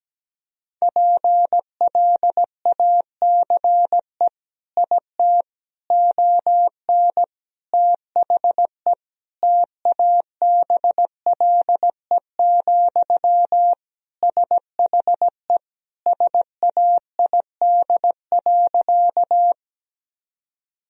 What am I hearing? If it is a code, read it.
Morse code, heard as a keyed tone, about 17 wpm, 700 Hz. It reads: PLACE IT ON THE TABLE, SHE SAID.
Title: Sentences from Top 1000 Words - Review Part 2 - 17wpm